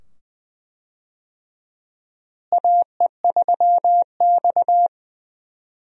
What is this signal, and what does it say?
Morse code, heard as a keyed tone, about 21 wpm, 700 Hz. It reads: AE3X
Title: USA Call Signs - 4 Characters - Repeated 2x - 20wpm